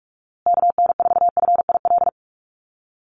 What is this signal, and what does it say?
Morse code, heard as a keyed tone, about 44 wpm, 700 Hz. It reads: KN4FIL
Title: USA Call Signs Repeated 3x 45wpm